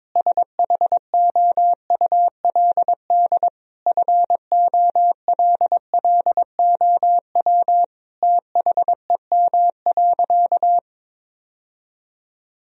Morse code, heard as a keyed tone, about 22 wpm, 700 Hz. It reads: SHOULD FOLLOW THEM.
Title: Sentences from Top 1000 Words - Review Part 1 - 22wpm